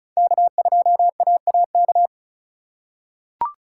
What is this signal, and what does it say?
Morse code, heard as a keyed tone, about 36 wpm, 700 Hz. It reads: K2AAK
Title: USA Call Signs - 5 Characters - 2x Character Spacing - 35wpm